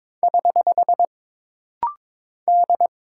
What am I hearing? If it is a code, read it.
Morse code, heard as a keyed tone, about 22 wpm, 700 Hz. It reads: <HH> D
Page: This rapid-fire timing is geared to push your ICR (Instant Character Recognition) skills.